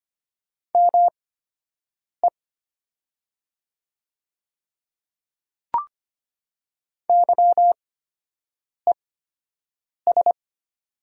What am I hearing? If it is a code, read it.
Morse code, heard as a keyed tone, about 25 wpm, spaced as for 7 wpm, 690 Hz. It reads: ME YES
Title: 8x Character Spacing - Top 500 Words - 25wpm